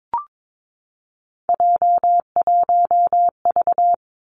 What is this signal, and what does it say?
Morse code, heard as a keyed tone, about 22 wpm, 700 Hz. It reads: J1V
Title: Three Challenging Letter-Number Combinations 22wpm